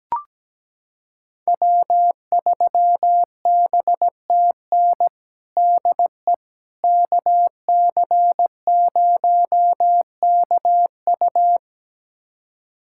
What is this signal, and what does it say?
Morse code, heard as a keyed tone, about 17 wpm, 700 Hz. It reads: W3BTN DE KC0KU
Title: USA Callsign DE Callsign - 5 Characters - 17wpm